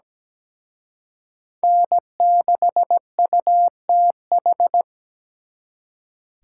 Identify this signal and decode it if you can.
Morse code, heard as a keyed tone, about 20 wpm, 700 Hz. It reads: N6UTH
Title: Speed Racing - USA 1x3 Call Signs 17wpm